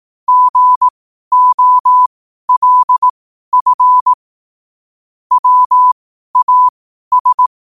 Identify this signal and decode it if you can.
Morse code, heard as a keyed tone, about 18 wpm, spaced as for 12 wpm, 1000 Hz. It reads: GOLF WAS
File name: qotdfiles / 12wpm